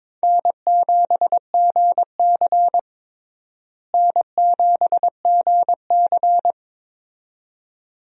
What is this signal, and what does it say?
Morse code, heard as a keyed tone, about 22 wpm, 700 Hz. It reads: N7GC N7GC
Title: USA Call Signs - 4 Characters - Repeated 2x - 22wpm